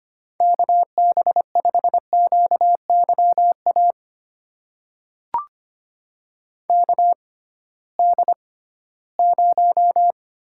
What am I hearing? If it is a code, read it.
Morse code, heard as a keyed tone, about 25 wpm, spaced as for 16 wpm, 700 Hz. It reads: KB5QYA KD0
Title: USA Call Signs - 6 Characters - 6x Character Spacing - 25wpm